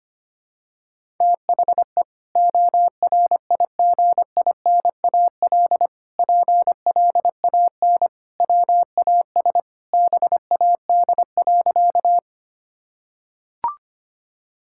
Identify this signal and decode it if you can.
Morse code, heard as a keyed tone, about 25 wpm, 700 Hz. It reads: THE ORIGINAL PLAN WAS BAD.
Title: Sentences from Top 1000 Words - Review Part 6 - 25wpm